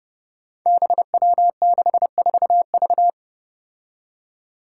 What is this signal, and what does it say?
Morse code, heard as a keyed tone, about 32 wpm, 700 Hz. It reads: BW64V
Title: Five Challenging Letter-Number Combinations 30wpm